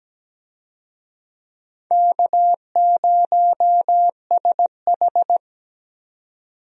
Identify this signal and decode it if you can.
Morse code, heard as a keyed tone, about 17 wpm, 700 Hz. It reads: K0SH
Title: USA Call Signs - 4 Characters - 17wpm